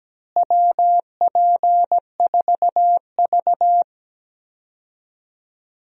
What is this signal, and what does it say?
Morse code, heard as a keyed tone, about 17 wpm, 700 Hz. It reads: WP4V
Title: USA Call Signs - 4 Characters - Repeated 3x - 17wpm